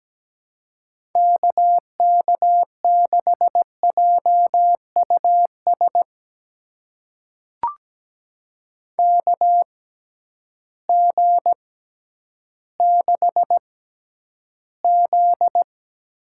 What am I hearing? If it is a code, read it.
Morse code, heard as a keyed tone, about 17 wpm, spaced as for 10 wpm, 700 Hz. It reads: KK6JUS KG6Z
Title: USA Call Signs - 6 Characters - 6x Character Spacing - 17wpm